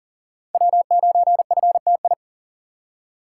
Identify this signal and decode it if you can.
Morse code, heard as a keyed tone, about 40 wpm, 700 Hz. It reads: W9FTI
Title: USA Call Signs - 5 Characters - 2x Character Spacing - 40wpm